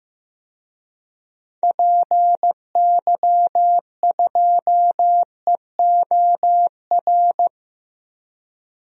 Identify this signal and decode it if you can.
Morse code, heard as a keyed tone, about 15 wpm, 700 Hz. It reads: PY2EOR
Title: DXCC-100 Call Signs - 6 Characters - Repeated 2x - 15wpm